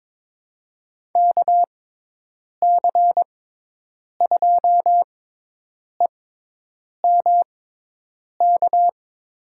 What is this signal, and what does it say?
Morse code, heard as a keyed tone, about 22 wpm, spaced as for 8 wpm, 700 Hz. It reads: KC2EMK
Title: USA Call Signs - 6 Characters - 6x Character Spacing - 22wpm